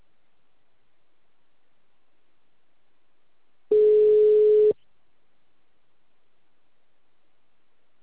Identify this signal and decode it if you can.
Ringback tone (Europe, 425 Hz).
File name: out- -2044- - - .52088